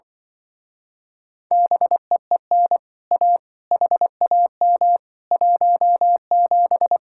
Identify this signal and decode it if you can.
Morse code, heard as a keyed tone, about 25 wpm, 700 Hz. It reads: BEEN A HAM 17